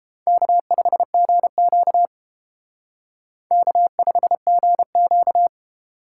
Morse code, heard as a keyed tone, about 30 wpm, 700 Hz. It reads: K5GQ K5GQ